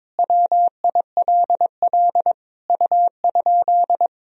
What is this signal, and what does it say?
Morse code, heard as a keyed tone, about 21 wpm, 700 Hz. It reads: WILL U?
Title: Sentences from Top 300 Words and Abbreviations 22wpm